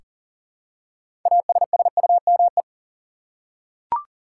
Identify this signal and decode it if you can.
Morse code, heard as a keyed tone, about 40 wpm, 700 Hz. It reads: ASSUME
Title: Six Letter Words - 40wpm